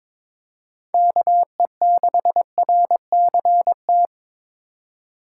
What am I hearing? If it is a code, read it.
Morse code, heard as a keyed tone, about 22 wpm, 700 Hz. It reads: KE6RCT
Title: USA Call Signs - 3x Character Spacing - 22wpm